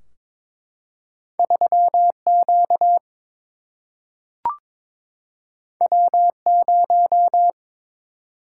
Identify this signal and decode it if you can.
Morse code, heard as a keyed tone, about 22 wpm, 700 Hz. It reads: 3Q W0
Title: Single-Letter Single-Number 22wpm